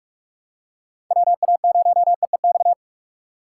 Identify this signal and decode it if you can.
Morse code, heard as a keyed tone, about 46 wpm, 700 Hz. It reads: WA0EEX
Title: USA Call Signs - 6 Characters - 3x Character Spacing - 45wpm